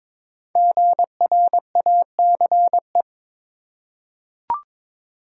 Morse code, heard as a keyed tone, about 22 wpm, 700 Hz. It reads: GRACE
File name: 5 Letter Words - First 3 Letters Given - 22wpm